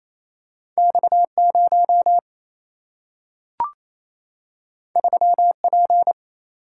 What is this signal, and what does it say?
Morse code, heard as a keyed tone, about 28 wpm, 700 Hz. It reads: X 0 3 P
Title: Single-Letter Single-Number 28wpm